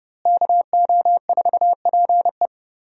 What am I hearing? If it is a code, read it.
Morse code, heard as a keyed tone, about 30 wpm, 700 Hz. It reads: KO4PE
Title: USA Call Signs - 5 Characters - Repeated 2x - 30wpm